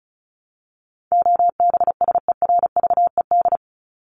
Morse code, heard as a keyed tone, about 34 wpm, 700 Hz. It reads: OBSERVED
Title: ED Suffix - Top 100 Words with Base Words 35wpm